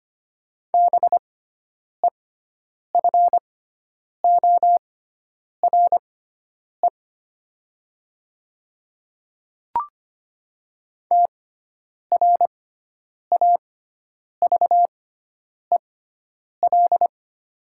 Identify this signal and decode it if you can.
Morse code, heard as a keyed tone, about 25 wpm, spaced as for 9 wpm, 700 Hz. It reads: BEFORE TRAVEL
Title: 6x Character Spacing - Top 500 Words - 25wpm